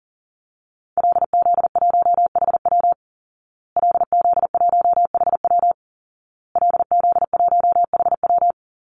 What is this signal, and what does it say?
Morse code, heard as a keyed tone, about 37 wpm, 700 Hz. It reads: LZ1HW LZ1HW LZ1HW
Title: Call Signs Repeated 2x - 40wpm